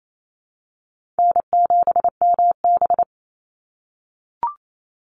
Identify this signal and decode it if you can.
Morse code, heard as a keyed tone, about 28 wpm, 700 Hz. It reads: N7MB
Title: Call Signs - 28wpm